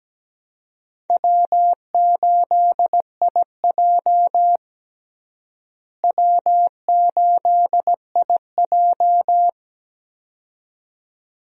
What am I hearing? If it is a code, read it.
Morse code, heard as a keyed tone, about 17 wpm, 700 Hz. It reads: W8IJ W8IJ
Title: USA Call Signs - 4 Characters - Repeated 2x - 17wpm